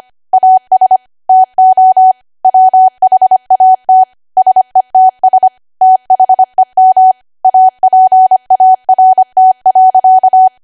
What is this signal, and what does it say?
Morse code, heard as a keyed tone, about 25 wpm, 750 Hz. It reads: AS TO WHAT SETS THEM APART.
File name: Archive / 25 WPM / 25WPM